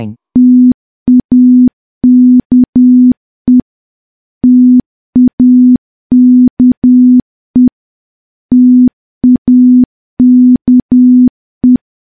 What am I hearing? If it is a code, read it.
Morse code, heard as a keyed tone, about 10 wpm, 250 Hz. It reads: TAKE TAKE TAKE
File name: morse / morse-training-10WPM